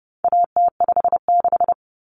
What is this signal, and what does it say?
Morse code, heard as a keyed tone, about 30 wpm, 700 Hz. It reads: AT56
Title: Common Cut Numbers - 4-Digit 30wpm